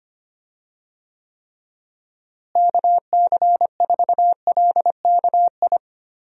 Morse code, heard as a keyed tone, about 25 wpm, 700 Hz. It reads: KC4LKI
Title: USA Call Signs - 6 Characters - Repeated 3x - 25wpm